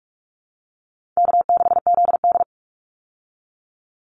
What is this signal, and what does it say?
Morse code, heard as a keyed tone, about 44 wpm, 700 Hz. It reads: K6ZD
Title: Easy Call Signs Repeated 3x - 45wpm